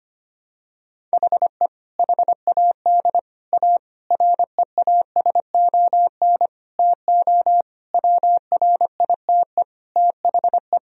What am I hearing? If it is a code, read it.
Morse code, heard as a keyed tone, about 25 wpm, 700 Hz. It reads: HE HAD A REASON TO WRITE THE
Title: Sentences from Top 1000 Words - Review Part 1 - 25wpm